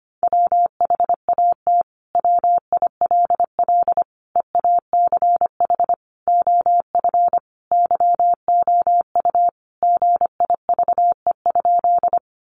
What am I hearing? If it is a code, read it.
Morse code, heard as a keyed tone, about 25 wpm, 700 Hz. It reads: WHAT WILL EACH OF YOU GIVE?
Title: Sentences from Top 200 Words 25wpm